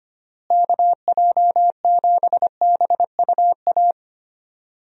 Morse code, heard as a keyed tone, about 25 wpm, 700 Hz. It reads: KJ7BUA
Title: USA Call Signs - 6 Characters - Repeated 3x - 25wpm